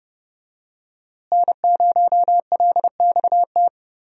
Morse code, heard as a keyed tone, about 28 wpm, 700 Hz. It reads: N0LXT
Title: USA Call Signs - 5 Characters - Repeated 3x - 30wpm